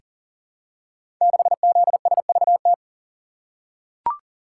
Morse code, heard as a keyed tone, about 40 wpm, 700 Hz. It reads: 6ZSVT
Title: Code Groups - 5 Letter:Numbers - 40wpm